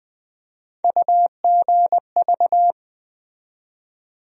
Morse code, heard as a keyed tone, about 20 wpm, 700 Hz. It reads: UGV
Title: USA Call Sign Postfixes - 3 Letters 20wpm